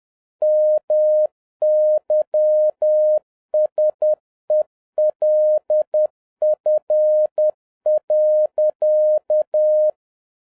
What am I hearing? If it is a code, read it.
Morse code, heard as a keyed tone, about 10 wpm, 610 Hz. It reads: MYSELF.